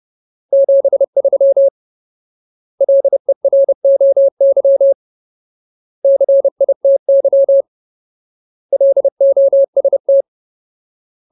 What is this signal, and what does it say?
Morse code, heard as a keyed tone, about 30 wpm, 550 Hz. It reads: 73 LEROY CITY LOST
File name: Ultimate Word List / 30WPM Ultimate